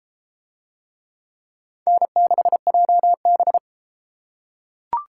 Morse code, heard as a keyed tone, about 31 wpm, 700 Hz. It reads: N6JB